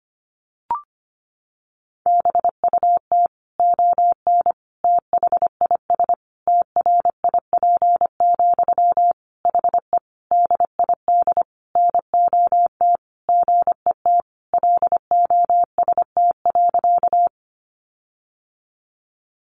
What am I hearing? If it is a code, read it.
Morse code, heard as a keyed tone, about 25 wpm, 700 Hz. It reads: BUT ON THIS TRIP, HE DID NOT GET LOST.
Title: Sentences from Top 1400 Words 25wpm